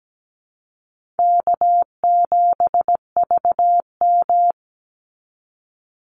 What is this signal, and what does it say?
Morse code, heard as a keyed tone, about 17 wpm, 700 Hz. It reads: K7VM
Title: Easy Call Signs Repeated 2x - 17wpm